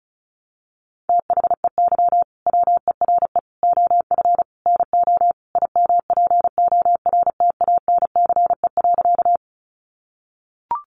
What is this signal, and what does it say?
Morse code, heard as a keyed tone, about 34 wpm, 700 Hz. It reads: THEY WERE OF NO IMPORTANCE.
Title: Sentences from Top 1200 Words 35wpm